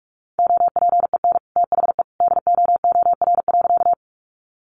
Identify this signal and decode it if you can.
Morse code, heard as a keyed tone, about 44 wpm, 700 Hz. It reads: OPEN THE DOOR.
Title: Sentences from Top 300 Words 45wpm